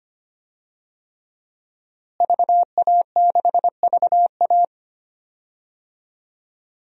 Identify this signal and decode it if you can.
Morse code, heard as a keyed tone, about 25 wpm, 700 Hz. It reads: VA6VA
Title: DXCC-100 Call Signs - 5 Characters - Repeated 3x - 25wpm